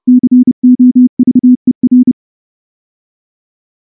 Morse code, heard as a keyed tone, about 28 wpm, 250 Hz. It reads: COVER
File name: morse / morse-training-30WPM